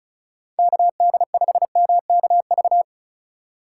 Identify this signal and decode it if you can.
Morse code, heard as a keyed tone, about 34 wpm, 700 Hz. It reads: KD5MKV
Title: USA Call Signs - 3x Character Spacing - 35wpm